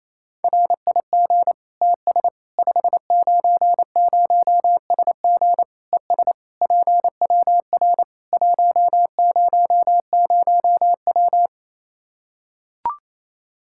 Morse code, heard as a keyed tone, about 28 wpm, 700 Hz. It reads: RIG TS 590SG ES PWR 100W
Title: Practice QSO - Exchange 2 - Info - 28wpm